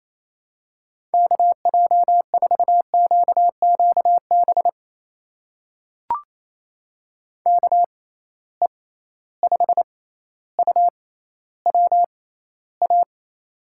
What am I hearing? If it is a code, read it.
Morse code, heard as a keyed tone, about 28 wpm, spaced as for 14 wpm, 700 Hz. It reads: KJ4QQB KE5UWA
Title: USA Call Signs - 6 Characters - 6x Character Spacing - 28wpm